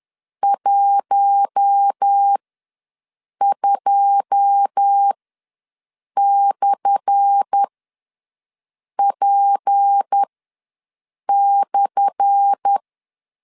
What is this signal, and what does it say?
Morse code, heard as a keyed tone, about 10 wpm, spaced as for 6 wpm, 800 Hz. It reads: 12/P/
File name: LEZIONI / 10 SIGNS / 10A SLASH ws 10wpm 4121